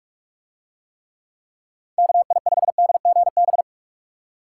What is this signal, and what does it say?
Morse code, heard as a keyed tone, about 45 wpm, 700 Hz. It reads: KI5DGB